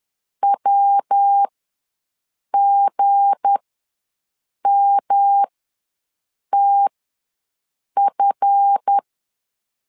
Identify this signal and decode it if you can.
Morse code, heard as a keyed tone, about 10 wpm, spaced as for 6 wpm, 800 Hz. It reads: WGMTF
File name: LEZIONI / 7 GF9Z / 7 GF ws 10wpm 3918